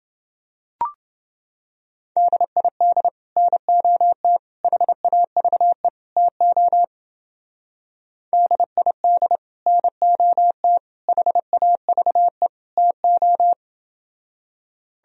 Morse code, heard as a keyed tone, about 29 wpm, 700 Hz. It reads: DID NOT HAVE TO DID NOT HAVE TO